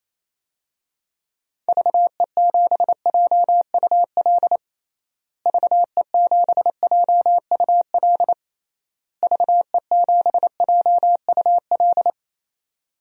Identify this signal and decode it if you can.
Morse code, heard as a keyed tone, about 28 wpm, 700 Hz. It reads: VE7JUL VE7JUL VE7JUL
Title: DXCC-100 Call Signs - 6 Characters - Repeated 3x - 28wpm